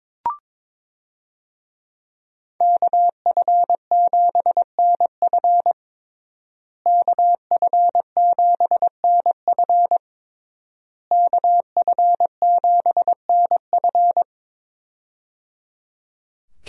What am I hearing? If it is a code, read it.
Morse code, heard as a keyed tone, about 22 wpm, 700 Hz. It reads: KF7NF KF7NF KF7NF
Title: USA Call Signs - 5 Characters - Repeated 3x - 22wpm